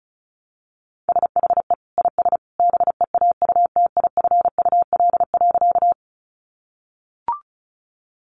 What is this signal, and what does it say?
Morse code, heard as a keyed tone, about 34 wpm, 700 Hz. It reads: SHE IS BEAUTIFUL.
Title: Sentences from Top 900 Words 35wpm